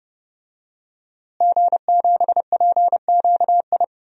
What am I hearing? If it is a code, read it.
Morse code, heard as a keyed tone, about 30 wpm, 700 Hz. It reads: G7PQI
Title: DXCC-100 Call Signs - 5 Characters - Repeated 2x - 30wpm